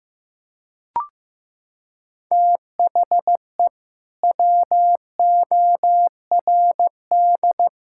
Morse code, heard as a keyed tone, about 16 wpm, 700 Hz. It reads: THE WORD
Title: Sentences from Top 1000 Words - Review Part 5 - 15wpm